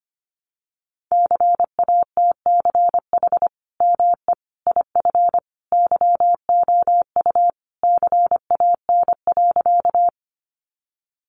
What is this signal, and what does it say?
Morse code, heard as a keyed tone, about 25 wpm, 700 Hz. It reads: CATCH ME IF YOU CAN.
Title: Sentences from Top 1500 Words 25wpm